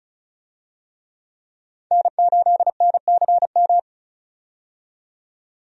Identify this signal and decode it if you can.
Morse code, heard as a keyed tone, about 34 wpm, 700 Hz. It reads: N8NCM
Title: USA Call Signs - 5 Characters - 35wpm